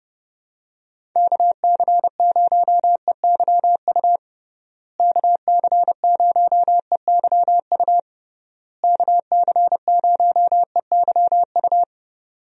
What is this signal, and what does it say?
Morse code, heard as a keyed tone, about 30 wpm, 700 Hz. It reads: KC0EYU KC0EYU KC0EYU
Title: USA Call Signs - 6 Characters - Repeated 3x - 30wpm